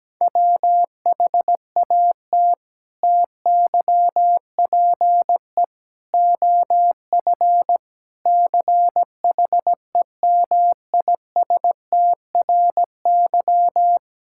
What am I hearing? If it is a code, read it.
Morse code, heard as a keyed tone, about 17 wpm, 700 Hz. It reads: WHAT TYPE OF CHEMISTRY
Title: Sentences from Top 1000 Words - Review Part 6 - 17wpm